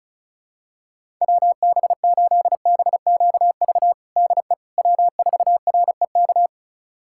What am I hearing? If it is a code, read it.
Morse code, heard as a keyed tone, about 34 wpm, 700 Hz. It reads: WB8BQV DE W4REK
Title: USA Callsign DE Callsign 35wpm